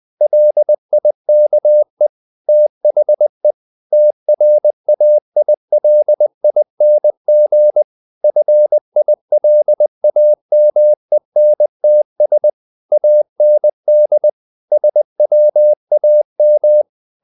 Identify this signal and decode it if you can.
Morse code, heard as a keyed tone, about 20 wpm, 600 Hz. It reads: LIKE THE TRAILING FILAMENTS AND SWAM
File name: the old man and the sea / the old man and the sea part 1 20wpm